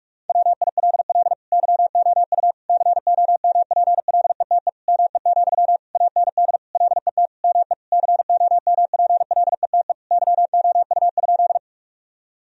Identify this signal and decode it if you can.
Morse code, heard as a keyed tone, about 46 wpm, 700 Hz. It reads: WILL YOU COMPLETE ME, AND LET ME COMPLETE YOU?
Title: Sentences from Top 1000 Words - Review Part 2 - 45wpm